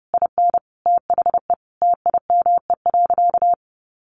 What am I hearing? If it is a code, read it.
Morse code, heard as a keyed tone, about 30 wpm, 700 Hz. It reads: IN THE TIME.
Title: Sentences from Top 200 Words 30wpm